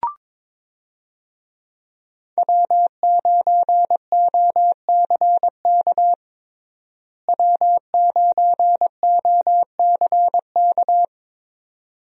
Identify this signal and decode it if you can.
Morse code, heard as a keyed tone, about 21 wpm, 700 Hz. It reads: W9OCK W9OCK
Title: USA Call Signs - 5 Characters - Repeated 2x - 22wpm